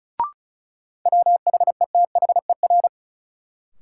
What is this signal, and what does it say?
Morse code, heard as a keyed tone, about 36 wpm, 700 Hz. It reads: WHETHER
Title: Sentences from Top 1000 Words - Review Part 3 - New Word Drill - 35wpm